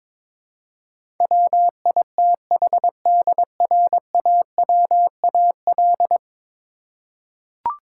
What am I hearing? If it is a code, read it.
Morse code, heard as a keyed tone, about 22 wpm, 700 Hz. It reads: WITHDRAWAL
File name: practice / 22wpm / 10 Letter Words - First 3 Letters Given - 22wpm